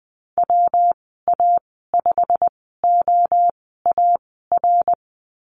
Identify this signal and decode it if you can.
Morse code, heard as a keyed tone, about 20 wpm, spaced as for 15 wpm, 700 Hz. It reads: WA5OAR
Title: 161 Lesson 26g - Letter g - Call Signs 20wpm